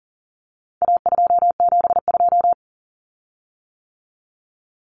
Morse code, heard as a keyed tone, about 40 wpm, 700 Hz. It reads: A272
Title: Common Cut Numbers - 4-Digit 40wpm